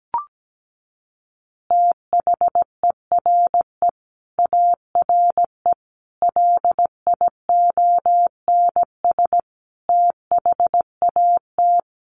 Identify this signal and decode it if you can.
Morse code, heard as a keyed tone, about 17 wpm, 700 Hz. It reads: THERE ARE LIONS THAT
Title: Easy Phrases 17wpm